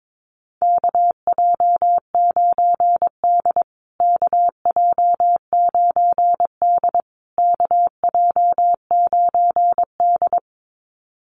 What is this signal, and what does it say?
Morse code, heard as a keyed tone, about 22 wpm, 700 Hz. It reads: KJ9D KJ9D KJ9D
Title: USA Call Signs Repeated 3x 22wpm